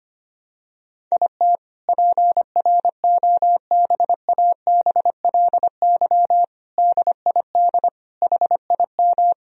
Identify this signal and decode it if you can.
Morse code, heard as a keyed tone, about 25 wpm, 700 Hz. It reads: IT PROBABLY DID HIM
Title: Sentences from Top 1000 Words - Review Part 4 - 25wpm